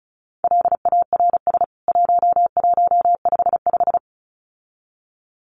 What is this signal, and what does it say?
Morse code, heard as a keyed tone, about 34 wpm, 700 Hz. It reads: LARS 1155
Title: CWT Exchange 35wpm